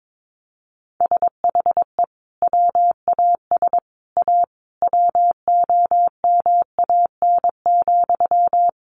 Morse code, heard as a keyed tone, about 22 wpm, 700 Hz. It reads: SHE WAS A WOMAN,
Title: Sentences from Top 1400 Words 22wpm